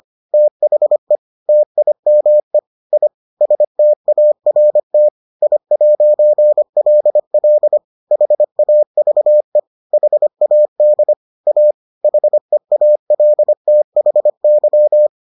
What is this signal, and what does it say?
Morse code, heard as a keyed tone, about 25 wpm, 600 Hz. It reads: THE TIME I START I'LL HAVE HAD A HEALTHY